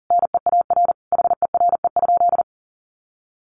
Morse code, heard as a keyed tone, about 40 wpm, 700 Hz. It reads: NEAR HERE?
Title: Sentences from Top 1200 Words 40wpm